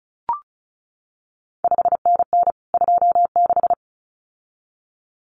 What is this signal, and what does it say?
Morse code, heard as a keyed tone, about 36 wpm, 700 Hz. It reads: 5NN 26
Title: CQ WW DX Contest Exchange Encore - 35wpm